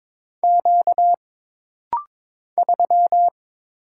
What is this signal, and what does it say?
Morse code, heard as a keyed tone, about 22 wpm, 700 Hz. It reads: Q 3
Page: This rapid-fire timing is geared to push your ICR (Instant Character Recognition) skills.